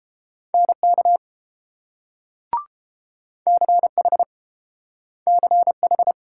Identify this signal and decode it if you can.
Morse code, heard as a keyed tone, about 30 wpm, 700 Hz. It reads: NK CH CH